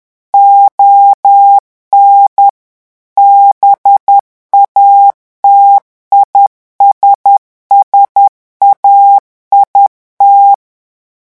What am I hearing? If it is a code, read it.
Morse code, heard as a keyed tone, about 10 wpm, 800 Hz. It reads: ON BATISSAIT
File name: morse mp3 / revision francais 10 mots minute